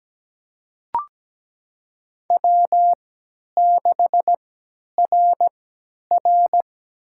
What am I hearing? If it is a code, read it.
Morse code, heard as a keyed tone, about 17 wpm, spaced as for 10 wpm, 700 Hz. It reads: W6RR
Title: USA Call Signs - 4 Characters - 3x Character Spacing - 17wpm